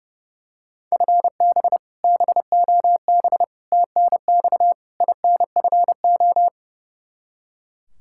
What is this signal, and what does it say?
Morse code, heard as a keyed tone, about 31 wpm, 700 Hz. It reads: FB BOB TNX INFO
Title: Practice QSO - Exchange 3 - More Info - 30wpm